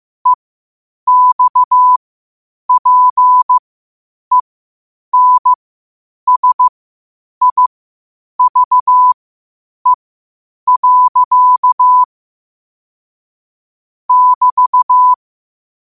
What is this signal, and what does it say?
Morse code, heard as a keyed tone, about 15 wpm, spaced as for 8 wpm, 1000 Hz. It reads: EXPENSIVE. <BT>
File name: qotdfiles / 8wpm